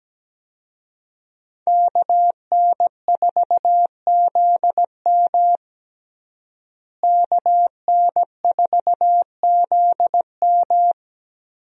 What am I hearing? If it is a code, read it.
Morse code, heard as a keyed tone, about 17 wpm, 700 Hz. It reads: KN4ZM KN4ZM
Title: USA Call Signs - 5 Characters - Repeated 2x - 17wpm